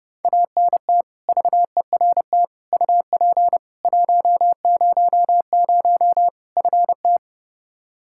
Morse code, heard as a keyed tone, about 30 wpm, 700 Hz. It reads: ANT VERT UP 100 FT
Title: Practice QSO - Exchange 2 - Info - 30wpm